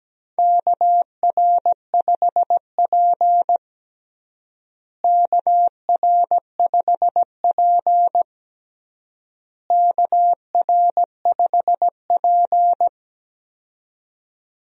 Morse code, heard as a keyed tone, about 17 wpm, 700 Hz. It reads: KR5P KR5P KR5P
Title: USA Call Signs - 4 Characters - Repeated 3x - 17wpm